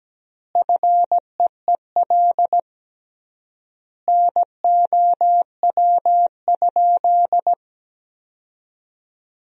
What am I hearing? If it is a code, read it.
Morse code, heard as a keyed tone, about 17 wpm, 700 Hz. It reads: FEEL NOW?
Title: 3x Word-Spacing - Sentences from Top 400 Words - 17wpm